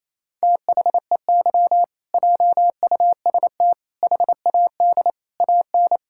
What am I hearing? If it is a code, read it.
Morse code, heard as a keyed tone, about 28 wpm, 700 Hz. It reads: THEY JUST HAD AN
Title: Sentences from Top 1000 Words - Review Part 6 - 28wpm